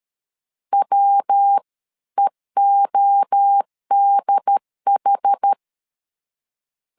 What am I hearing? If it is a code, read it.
Morse code, heard as a keyed tone, about 13 wpm, 800 Hz. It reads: WEODH
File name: CW / DET / 1 3 1500C S B 12wpm 2828